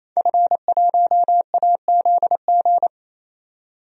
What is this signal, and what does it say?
Morse code, heard as a keyed tone, about 28 wpm, 700 Hz. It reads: F1AZG
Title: DXCC-100 Call Signs - 5 Characters - 28wpm